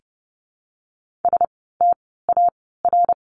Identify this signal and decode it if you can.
Morse code, heard as a keyed tone, about 30 wpm, 700 Hz. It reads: S T A R
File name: Lesson 16f - Letter c - Words 30wpm